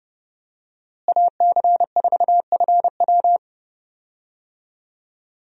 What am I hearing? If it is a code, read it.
Morse code, heard as a keyed tone, about 30 wpm, 700 Hz. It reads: AC4FW
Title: USA Call Signs - 5 Characters - Repeated 3x - 30wpm